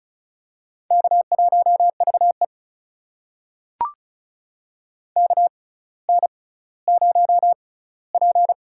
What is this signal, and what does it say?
Morse code, heard as a keyed tone, about 36 wpm, spaced as for 18 wpm, 700 Hz. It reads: K1VE KN0P
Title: USA Call Signs - 6x Character Spacing - 35wpm